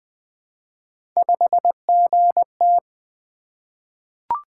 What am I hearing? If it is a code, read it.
Morse code, heard as a keyed tone, about 20 wpm, 700 Hz. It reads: 5GT
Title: Three Letter-Number Combinations - 20wpm